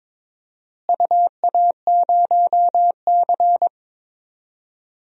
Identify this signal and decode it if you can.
Morse code, heard as a keyed tone, about 22 wpm, 700 Hz. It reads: UA0C
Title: DXCC-100 Call Signs - 4 Characters - Repeated 2x - 22wpm